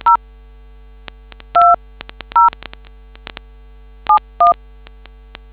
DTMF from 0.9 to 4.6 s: 20 71